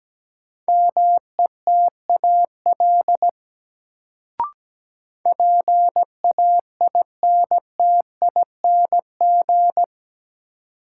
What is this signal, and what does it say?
Morse code, heard as a keyed tone, about 17 wpm, 700 Hz. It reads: METAL PAINTING
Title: Sentences from Top 1000 Words - Review Part 6 - New Word Drill - 17wpm